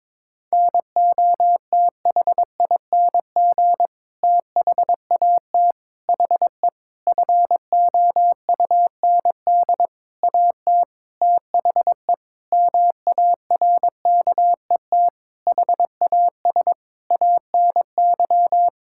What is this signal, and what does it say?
Morse code, heard as a keyed tone, about 22 wpm, 700 Hz. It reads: NOTHING THAT HE FOUND AT THE MARKET HAS ANY
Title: Sentences from Top 1000 Words - Review Part 6 - 22wpm